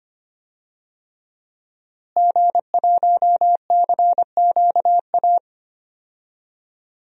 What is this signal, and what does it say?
Morse code, heard as a keyed tone, about 25 wpm, 700 Hz. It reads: G1CQA
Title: DXCC-100 Call Signs - 5 Characters - Repeated 3x - 25wpm